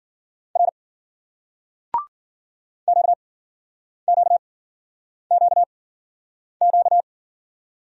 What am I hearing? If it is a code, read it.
Morse code, heard as a keyed tone, about 46 wpm, spaced as for 9 wpm, 700 Hz. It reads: R QQQQ
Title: Speed-Racing Single Letters 40wpm